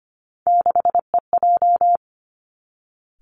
Morse code, heard as a keyed tone, about 25 wpm, 700 Hz. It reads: 6EJ
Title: Three Challenging Letter-Number Combinations 25wpm